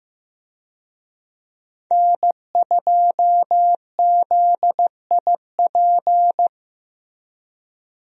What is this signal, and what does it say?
Morse code, heard as a keyed tone, about 15 wpm, 700 Hz. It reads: N2ZIP
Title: USA Call Signs - 5 Characters - Repeated 3x - 15wpm